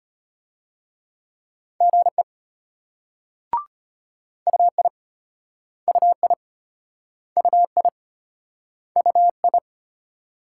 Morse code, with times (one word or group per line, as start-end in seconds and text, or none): 1.8-2.2 GE
4.5-4.9 UI
5.9-6.3 UI
7.4-7.9 UI
9.0-9.6 UI